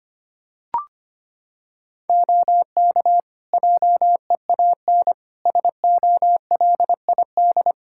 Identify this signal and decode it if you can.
Morse code, heard as a keyed tone, about 25 wpm, 700 Hz. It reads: OK JEAN SOLID
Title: Practice QSO - Exchange 3 - More Info - 25wpm